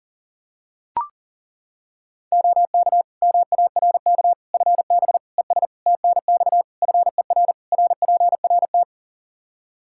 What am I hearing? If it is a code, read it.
Morse code, heard as a keyed tone, about 40 wpm, 700 Hz. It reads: OK MARK FB ES TNX FER RPRT
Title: Practice QSO - Exchange 2 - Info - 40wpm